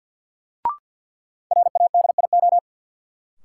Morse code, heard as a keyed tone, about 52 wpm, 700 Hz. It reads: RADIO
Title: Sentences from Top 1000 Words - Review Part 3 - New Word Drill - 50wpm